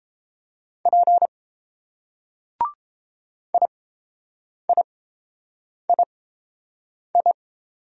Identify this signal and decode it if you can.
Morse code, heard as a keyed tone, about 30 wpm, spaced as for 7 wpm, 700 Hz. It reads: P IIII